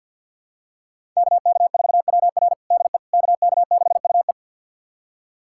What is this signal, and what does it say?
Morse code, heard as a keyed tone, about 47 wpm, 700 Hz. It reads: KK4WR DE KK6UE
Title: USA Callsign DE Callsign - 5 Characters - 50wpm